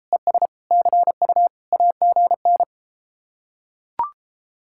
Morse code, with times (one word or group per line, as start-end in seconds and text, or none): 0.1-0.5 ES
0.7-1.5 CU
1.7-2.6 AGN